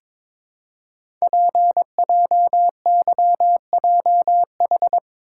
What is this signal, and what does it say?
Morse code, heard as a keyed tone, about 22 wpm, 700 Hz. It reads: PJYJH
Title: Code Groups - 5 Letters - 22wpm